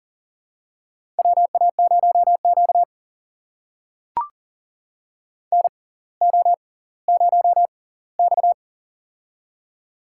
Morse code, heard as a keyed tone, about 39 wpm, spaced as for 20 wpm, 690 Hz. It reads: WA0Q NO0X
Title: USA Call Signs - 4 Characters - 6x Character Spacing - 40wpm